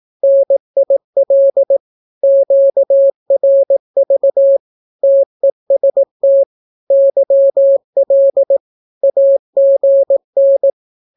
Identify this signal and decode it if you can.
Morse code, heard as a keyed tone, about 18 wpm, 550 Hz. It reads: NIL QRV TEST YL AGN
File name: naukapodstaw / skroty / skroty